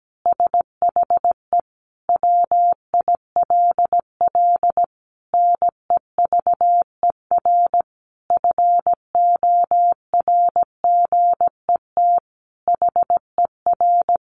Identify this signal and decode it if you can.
Morse code, heard as a keyed tone, about 17 wpm, 700 Hz. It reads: SHE WILL NEVER FORGET HER
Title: Additional Easy Phrases 17wpm